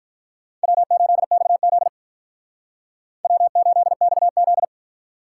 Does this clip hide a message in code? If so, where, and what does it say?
Yes: Morse code from 0.6 to 4.6 s, heard as a keyed tone, about 45 wpm, 700 Hz. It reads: W8XZ W8XZ